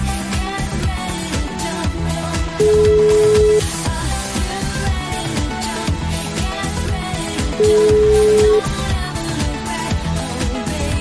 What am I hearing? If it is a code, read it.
Ringback tone (Europe, 425 Hz), with noise.